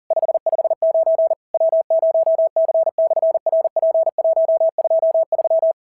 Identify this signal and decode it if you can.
Morse code, heard as a keyed tone, about 41 wpm, 650 Hz. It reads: 559 W0C/RP123
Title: SOTA QSO Practice 40wpm